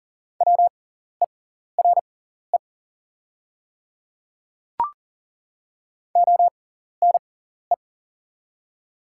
Morse code, heard as a keyed tone, about 40 wpm, spaced as for 14 wpm, 700 Hz. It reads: WERE ONE
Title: 6x Character Spacing - Top 100 Words - 40wpm